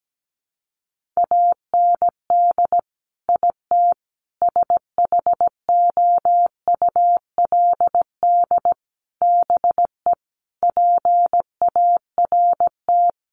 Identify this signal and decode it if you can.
Morse code, heard as a keyed tone, about 17 wpm, 700 Hz. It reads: AND IT SHOULD BE PART
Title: Sentences from Top 1500 Words 17wpm